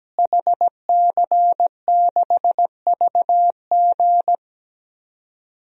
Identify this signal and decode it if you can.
Morse code, heard as a keyed tone, about 17 wpm, 700 Hz. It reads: HC6VG
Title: DXCC-100 Call Signs - 5 Characters - Repeated 3x - 17wpm